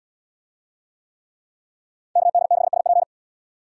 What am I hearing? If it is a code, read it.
Morse code, heard as a keyed tone, about 73 wpm, 700 Hz. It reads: KD6SP